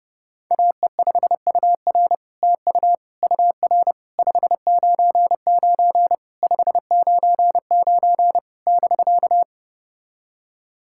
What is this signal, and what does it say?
Morse code, heard as a keyed tone, about 30 wpm, 700 Hz. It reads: AE5UR TU UR 599 599 <BK>
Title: SOTA QSO Practice 30wpm